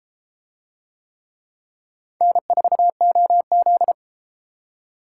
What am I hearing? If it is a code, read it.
Morse code, heard as a keyed tone, about 32 wpm, 700 Hz. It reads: N4OZ